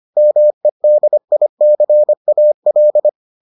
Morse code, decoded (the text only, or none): MEDICAL